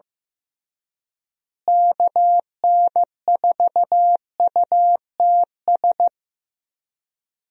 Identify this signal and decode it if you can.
Morse code, heard as a keyed tone, about 15 wpm, 700 Hz. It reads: KN4UTS
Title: USA Call Signs - 6 Characters - Repeated 3x - 15wpm